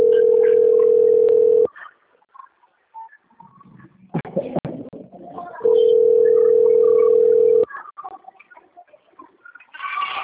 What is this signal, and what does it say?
Ringback tone (North America), 440 + 480 Hz.